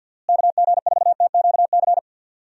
Morse code, heard as a keyed tone, about 52 wpm, 700 Hz. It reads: KG4TQC
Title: USA Call Signs - 6 Characters - Repeated 2x - 50wpm